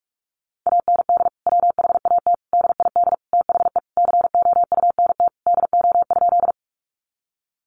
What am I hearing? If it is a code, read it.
Morse code, heard as a keyed tone, about 44 wpm, 700 Hz. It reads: AND WHAT DID THE COUNT DO?
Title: Sentences from Top 1500 Words 45wpm